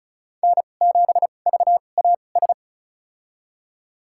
Morse code, heard as a keyed tone, about 34 wpm, spaced as for 25 wpm, 700 Hz. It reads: N7VAS
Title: USA Call Signs - 2x Character Spacing - 35wpm